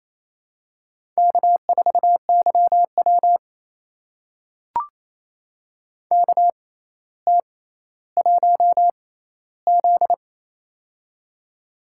Morse code, heard as a keyed tone, about 28 wpm, spaced as for 14 wpm, 700 Hz. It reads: K4YW KT1Z
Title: USA Call Signs - 4 Characters - 6x Character Spacing - 28wpm